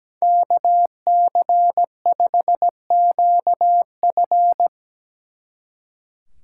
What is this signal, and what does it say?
Morse code, heard as a keyed tone, about 17 wpm, 700 Hz. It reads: KC5QF
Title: Speed Racing - USA 2x2 Call Signs 17wpm